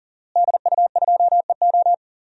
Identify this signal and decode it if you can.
Morse code, heard as a keyed tone, about 40 wpm, 700 Hz. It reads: DU2EO
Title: DXCC-100 Call Signs - 5 Characters - Repeated 3x - 40wpm